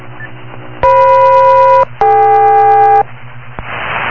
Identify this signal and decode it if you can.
Aircraft SELCAL, heard as two pairs of tones. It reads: FM-DK